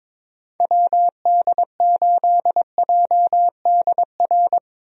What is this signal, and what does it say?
Morse code, heard as a keyed tone, about 22 wpm, 700 Hz. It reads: WD8JDR
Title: USA Call Signs - 6 Characters - Repeated 3x - 22wpm